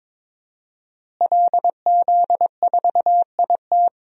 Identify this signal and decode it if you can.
Morse code, heard as a keyed tone, about 22 wpm, 700 Hz. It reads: LZ4IT
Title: DXCC-100 Call Signs - 5 Characters - Repeated 2x - 22wpm